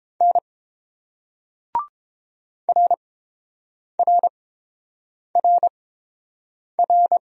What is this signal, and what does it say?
Morse code, heard as a keyed tone, about 30 wpm, spaced as for 7 wpm, 690 Hz. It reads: N RRRR